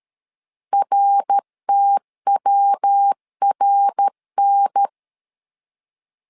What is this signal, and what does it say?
Morse code, heard as a keyed tone, about 13 wpm, 800 Hz. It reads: RTWRN
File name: LEZIONI / 4 Q 12wpm 1833